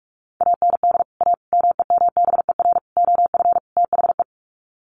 Morse code, heard as a keyed tone, about 44 wpm, 700 Hz. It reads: AND A MEMBER OF THE
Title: Sets of 5 Words 45wpm